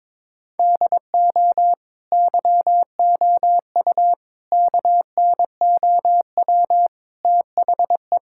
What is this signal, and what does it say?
Morse code, heard as a keyed tone, about 22 wpm, 700 Hz. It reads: DO YOU KNOW THE
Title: Sentences from Top 1000 Words - Review Part 6 - 22wpm